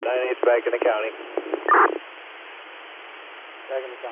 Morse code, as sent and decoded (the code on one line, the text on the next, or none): none